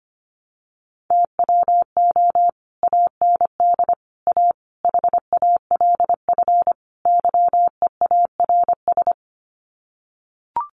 Morse code, heard as a keyed tone, about 25 wpm, 700 Hz. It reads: TWO AND A HALF YEARS
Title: Sets of 5 Words 25wpm